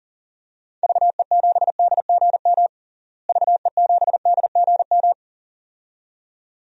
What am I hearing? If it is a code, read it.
Morse code, heard as a keyed tone, about 40 wpm, 700 Hz. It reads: VE7DGM VE7DGM
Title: DXCC-100 Call Signs - 6 Characters - Repeated 2x - 40wpm